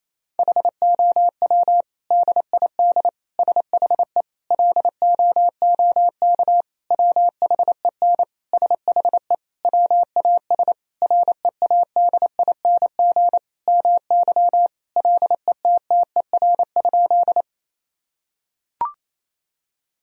Morse code, heard as a keyed tone, about 28 wpm, 700 Hz. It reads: HOW DID SHE LOOK WHEN SHE WAS READING MY LETTER?
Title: Sentences from Top 1000 Words - Review Part 3 - 28wpm